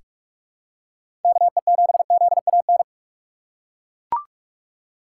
Morse code, heard as a keyed tone, about 41 wpm, 700 Hz. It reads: KE7ZAN
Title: USA Call Signs - 6x Character Spacing - 45wpm